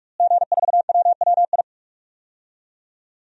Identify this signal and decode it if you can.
Morse code, heard as a keyed tone, about 46 wpm, 700 Hz. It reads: G4WWI
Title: DXCC-100 Call Signs - 5 Characters - Repeated 3x - 45wpm